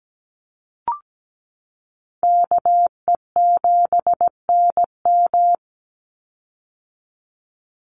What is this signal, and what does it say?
Morse code, heard as a keyed tone, about 17 wpm, 700 Hz. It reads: KE7NM
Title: USA Call Signs 17wpm